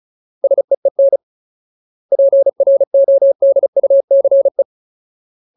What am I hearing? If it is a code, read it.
Morse code, heard as a keyed tone, about 34 wpm, 550 Hz. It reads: SEEN PRODUCE
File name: Ultimate Word List / 35WPM Ultimate